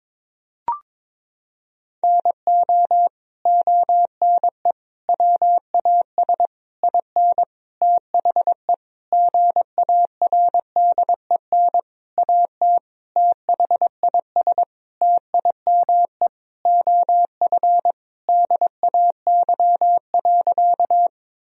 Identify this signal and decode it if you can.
Morse code, heard as a keyed tone, about 22 wpm, 700 Hz. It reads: NO ONE WAS IN THE GARDEN AT THIS TIME OF DAY.
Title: Sentences from Top 1000 Words - Review Part 4 - 22wpm